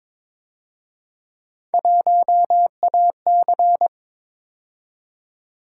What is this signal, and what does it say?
Morse code, heard as a keyed tone, about 26 wpm, 700 Hz. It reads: 1AC
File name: Speed-Racing - Three Letter-Number Combinations - 22wpm